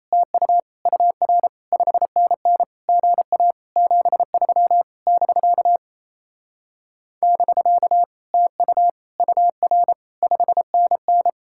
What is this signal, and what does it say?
Morse code, heard as a keyed tone, about 38 wpm, 700 Hz. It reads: TU UR 5NN GA 73 <BK> <BK> TU UR 5NN